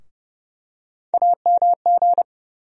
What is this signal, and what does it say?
Morse code, heard as a keyed tone, about 30 wpm, 700 Hz. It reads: AMG
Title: USA Call Sign Postfixes - 3 Letters 30wpm